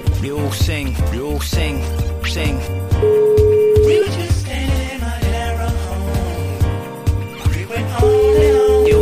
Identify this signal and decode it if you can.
Ringback tone (Europe, 425 Hz), with speech.